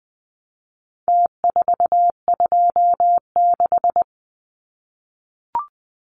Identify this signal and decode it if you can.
Morse code, heard as a keyed tone, about 20 wpm, 700 Hz. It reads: T426
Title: Numbers - Contest Style - 4-Digit 20wpm